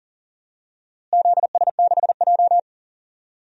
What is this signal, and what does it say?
Morse code, heard as a keyed tone, about 40 wpm, 700 Hz. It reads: ZS6J
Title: DXCC-100 Call Signs - 4 Characters - Repeated 3x - 40wpm